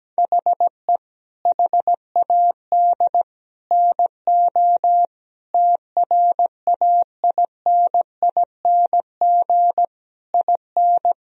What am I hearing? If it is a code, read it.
Morse code, heard as a keyed tone, about 17 wpm, 700 Hz. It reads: HE HAD NO TRAINING IN
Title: Sentences from Top 1000 Words - Review Part 4 - 17wpm